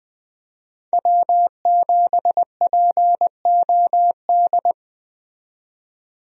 Morse code, heard as a keyed tone, about 21 wpm, 700 Hz. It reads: W7POD
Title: USA Call Signs - 5 Characters - 20wpm